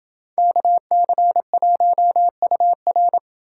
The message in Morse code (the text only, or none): KC1UR